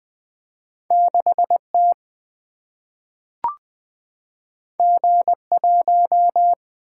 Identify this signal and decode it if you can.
Morse code, heard as a keyed tone, about 20 wpm, 700 Hz. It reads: 6 T G 1
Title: Single-Letter Single-Number 20wpm